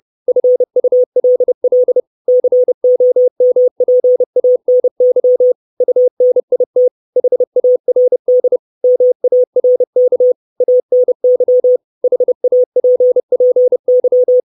Morse code, heard as a keyed tone, about 30 wpm, 500 Hz. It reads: FULL COMPANY UNIT HARD MARK ANY HAPPY
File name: 632 Words 30WPM